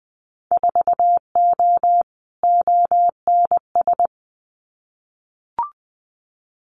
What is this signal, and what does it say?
Morse code, heard as a keyed tone, about 19 wpm, 700 Hz. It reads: 4O ONS
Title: Winter Field Day Exchanges 20wpm